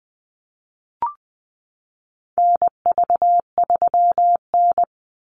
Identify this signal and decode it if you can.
Morse code, heard as a keyed tone, about 20 wpm, 700 Hz. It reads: NV3N
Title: Call Signs - 20wpm